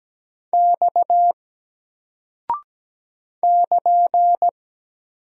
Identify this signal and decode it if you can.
Morse code, heard as a keyed tone, about 17 wpm, 700 Hz. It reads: X <KN>
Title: Single Character (Letter:Number:Punctuation:Prosign) 17wpm